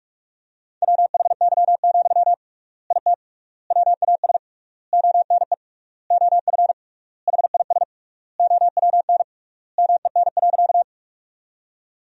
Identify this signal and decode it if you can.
Morse code, heard as a keyed tone, about 46 wpm, 700 Hz. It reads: WHY, IT WAS ONE OF HIS OWN MEN.
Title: 3x Word-Spacing - Sentences from Top 300 Words - 45wpm